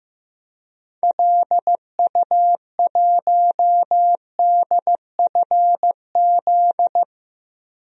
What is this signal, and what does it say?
Morse code, heard as a keyed tone, about 16 wpm, 700 Hz. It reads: LU1DFZ
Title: DXCC-100 Call Signs - 6 Characters - Repeated 2x - 15wpm